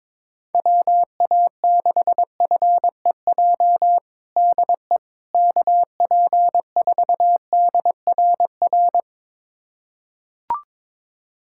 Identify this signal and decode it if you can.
Morse code, heard as a keyed tone, about 22 wpm, 700 Hz. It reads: WA6FEJ DE KP4DRR
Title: USA Callsign DE Callsign - 6 Characters - 22wpm